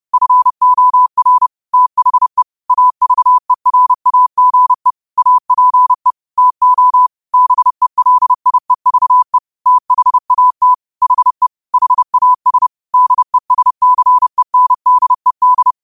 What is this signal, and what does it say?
Morse code, heard as a keyed tone, about 30 wpm, 1000 Hz. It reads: FOR THE AVERAGE APE TO BELIEVE THAT HE HAS DESCENDED
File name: qotdfiles / 30wpm